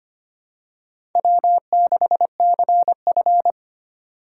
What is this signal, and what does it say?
Morse code, heard as a keyed tone, about 25 wpm, 700 Hz. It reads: W6CF
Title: USA Call Signs - 4 Characters - 3x Character Spacing - 25wpm